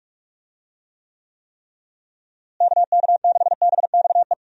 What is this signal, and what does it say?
Morse code, heard as a keyed tone, about 46 wpm, 700 Hz. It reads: KK6BXE
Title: USA Call Signs - 6 Characters - Repeated 3x - 45wpm